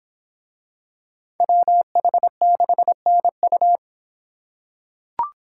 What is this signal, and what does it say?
Morse code, heard as a keyed tone, about 25 wpm, 700 Hz. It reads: WH6NU